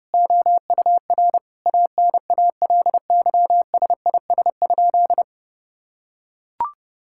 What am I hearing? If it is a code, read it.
Morse code, heard as a keyed tone, about 32 wpm, 700 Hz. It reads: OUR ANALYSIS?
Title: Sentences from Top 1000 Words - Review Part 5 - 30wpm